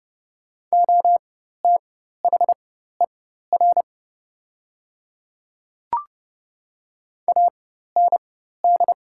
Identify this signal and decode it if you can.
Morse code, heard as a keyed tone, about 29 wpm, spaced as for 14 wpm, 700 Hz. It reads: OTHER AND
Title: 4x Character Spacing - Top 100 Words - 30wpm